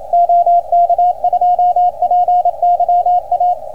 Morse code, heard as a keyed tone, about 27 wpm, 710 Hz, with radio noise. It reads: OK2PYA